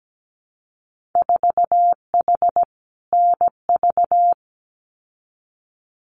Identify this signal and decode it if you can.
Morse code, heard as a keyed tone, about 17 wpm, 700 Hz. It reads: 4H NV
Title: Winter Field Day Exchanges Encore 17wpm